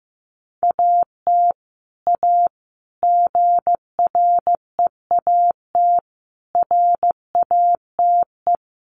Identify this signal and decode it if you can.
Morse code, heard as a keyed tone, about 15 wpm, 700 Hz. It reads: AT A GREAT RATE
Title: Sentences from Top 1200 Words 15wpm